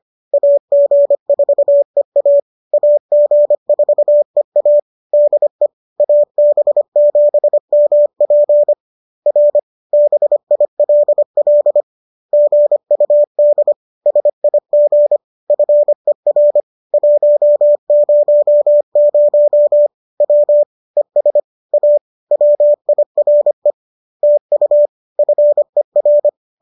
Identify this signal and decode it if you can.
Morse code, heard as a keyed tone, about 25 wpm, 590 Hz. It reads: AG4EA AG4EA DE AB7MP R BILL GUD SIG FER 100 W ES A WIRE TU FER
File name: qso206 25